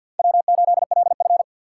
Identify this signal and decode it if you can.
Morse code, heard as a keyed tone, about 46 wpm, 700 Hz. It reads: W8LF
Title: USA Call Signs - 4 Characters - 6x Character Spacing - 50wpm